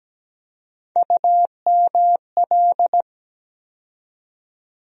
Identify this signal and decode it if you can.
Morse code, heard as a keyed tone, about 17 wpm, 700 Hz. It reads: UML
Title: USA Call Sign Postfixes - 3 Letters 17wpm